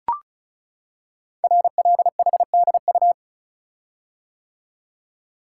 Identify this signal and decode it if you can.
Morse code, heard as a keyed tone, about 36 wpm, 690 Hz. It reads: RLHDU
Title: Code Groups - 5 Letters - 35wpm